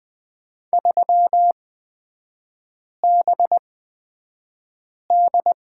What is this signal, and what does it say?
Morse code, heard as a keyed tone, about 20 wpm, spaced as for 5 wpm, 700 Hz. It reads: 3BD
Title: 236 Lesson 38e - Letter q - Three Characters 20wpm